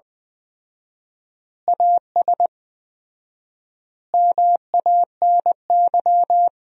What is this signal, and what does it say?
Morse code, heard as a keyed tone, about 20 wpm, 700 Hz. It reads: AS MANY
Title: 4x Word-Spacing - Sentences from Top 400 Words - 20wpm